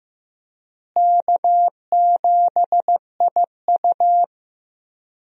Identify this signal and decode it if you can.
Morse code, heard as a keyed tone, about 15 wpm, 700 Hz. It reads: K7IU
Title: USA Call Signs - 4 Characters - Repeated 2x - 15wpm